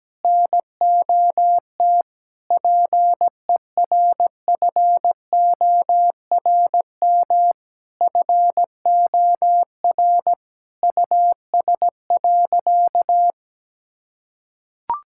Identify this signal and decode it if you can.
Morse code, heard as a keyed tone, about 17 wpm, 700 Hz. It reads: NOT PERFORM FOR US.
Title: Sentences from Top 1000 Words - Review Part 5 - 17wpm